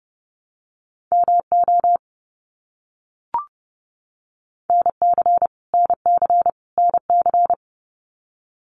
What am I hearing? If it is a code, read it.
Morse code, heard as a keyed tone, about 28 wpm, 700 Hz. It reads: MO NC NC NC
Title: US States Repeated 3x 30wpm